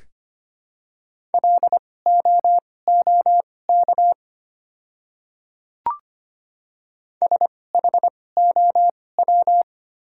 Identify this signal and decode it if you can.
Morse code, heard as a keyed tone, about 25 wpm, spaced as for 18 wpm, 700 Hz. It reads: LOOK SHOW
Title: 2x Character Spacing - Top 500 Words - 25wpm